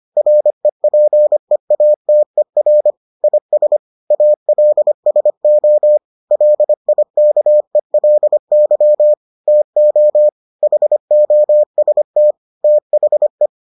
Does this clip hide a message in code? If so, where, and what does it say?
Yes: Morse code from 0.2 to 13.5 s, heard as a keyed tone, about 25 wpm, 600 Hz. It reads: REPEATER IS ALSO LIKELY TO HOST THE